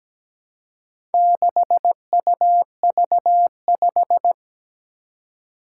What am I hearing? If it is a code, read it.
Morse code, heard as a keyed tone, about 17 wpm, 700 Hz. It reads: 6UV5
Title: Four Challenging Letter-Number Combinations 17wpm